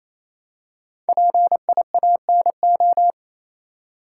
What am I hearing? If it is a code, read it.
Morse code, heard as a keyed tone, about 28 wpm, 700 Hz. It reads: PIANO
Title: Five Letter Words - 28wpm